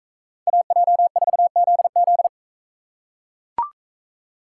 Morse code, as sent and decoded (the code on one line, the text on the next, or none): .- .--- ....- --.. --..
AJ4ZZ